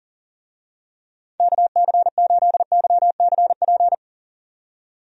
Morse code, heard as a keyed tone, about 40 wpm, 700 Hz. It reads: KC8YCP
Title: USA Call Signs - 6 Characters - 40wpm